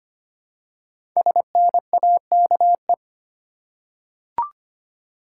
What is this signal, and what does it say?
Morse code, heard as a keyed tone, about 25 wpm, 700 Hz. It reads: SNAKE
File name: 5 Letter Words - First 3 Letters Given - 25wpm